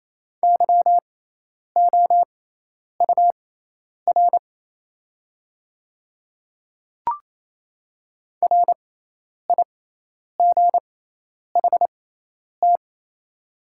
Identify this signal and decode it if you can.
Morse code, heard as a keyed tone, about 28 wpm, spaced as for 10 wpm, 700 Hz. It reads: YOUR RIGHT
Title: 6x Character Spacing - Top 300 Words - 28wpm